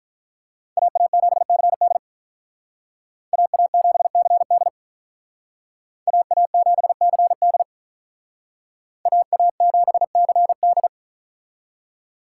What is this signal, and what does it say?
Morse code, heard as a keyed tone, about 44 wpm, 700 Hz. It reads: AA7CD AA7CD AA7CD AA7CD